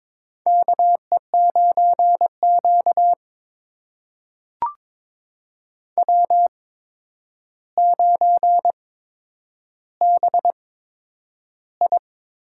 Morse code, heard as a keyed tone, about 22 wpm, spaced as for 9 wpm, 700 Hz. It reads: KE9Q W9BI
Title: USA Call Signs - 4 Characters - 8x Character Spacing - 22wpm